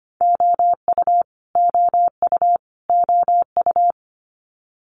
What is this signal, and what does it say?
Morse code, heard as a keyed tone, about 25 wpm, 700 Hz. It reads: OU OU OU
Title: Multi-letter Phonemes Repeated 3x 25wpm